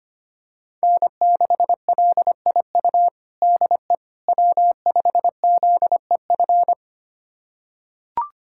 Morse code, heard as a keyed tone, about 25 wpm, 700 Hz. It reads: N6LIU DE W5ZEF
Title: USA Callsign DE Callsign - 5 Characters - 25wpm